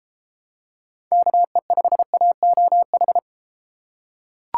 Morse code, heard as a keyed tone, about 32 wpm, 700 Hz. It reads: KE5AOH